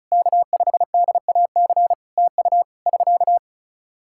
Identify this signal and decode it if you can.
Morse code, heard as a keyed tone, about 34 wpm, 700 Hz. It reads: K5DAC TU <SK>
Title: K1USN SST QSO Practice 35wpm